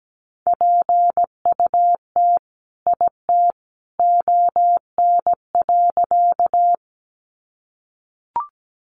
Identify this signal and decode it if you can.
Morse code, heard as a keyed tone, about 17 wpm, 700 Hz. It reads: PUT IT ON.
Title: Sentences from Top 200 Words 17wpm